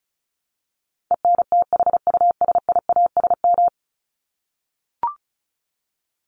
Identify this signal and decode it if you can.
Morse code, heard as a keyed tone, about 34 wpm, 700 Hz. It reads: ENTHUSIASM
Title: Top 1751-2000 Words 35wpm